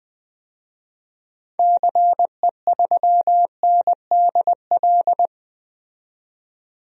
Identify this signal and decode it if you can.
Morse code, heard as a keyed tone, about 20 wpm, 700 Hz. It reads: CE3NDL
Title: DXCC-100 Call Signs - 6 Characters - 20wpm